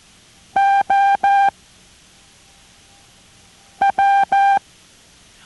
Morse code, heard as a keyed tone, about 14 wpm, spaced as for 3 wpm, 790 Hz. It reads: OW